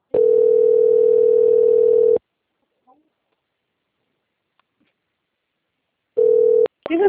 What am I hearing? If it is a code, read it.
Ringback tone (North America), 440 + 480 Hz, with speech.
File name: rg-1600-unknown- - - .8770